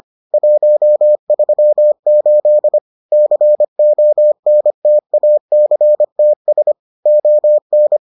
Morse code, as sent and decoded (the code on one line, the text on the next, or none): .---- ...-- ---.. / -.-. --- -. - .- -.-. - ... / --- -.
138 CONTACTS ON